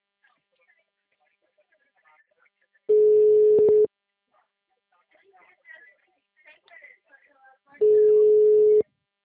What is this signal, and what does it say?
Ringback tone (Europe, 425 Hz), with speech.
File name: out- -2022- - - .47652